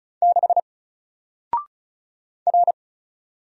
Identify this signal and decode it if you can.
Morse code, heard as a keyed tone, about 36 wpm, 700 Hz. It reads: 6 R
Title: Single Letter-Number 35wpm